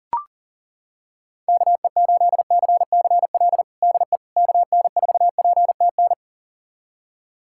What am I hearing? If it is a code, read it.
Morse code, heard as a keyed tone, about 40 wpm, 700 Hz. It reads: KE8CCL DE KN4PTN
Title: USA Callsign DE Callsign - 6 Characters - 40wpm